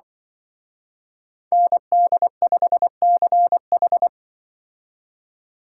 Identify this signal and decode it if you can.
Morse code, heard as a keyed tone, about 24 wpm, 700 Hz. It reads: ND5CH
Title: Speed Racing - USA 2x2 Call Signs 20wpm